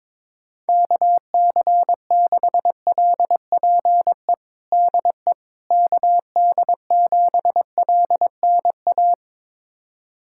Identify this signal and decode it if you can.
Morse code, heard as a keyed tone, about 22 wpm, 700 Hz. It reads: KC6LPE DE KD7LNA
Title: USA Callsign DE Callsign - 6 Characters - 22wpm